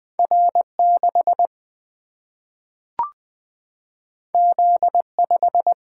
Morse code, heard as a keyed tone, about 20 wpm, 700 Hz. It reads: R 6 Z 5
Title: Single-Letter Single-Number 20wpm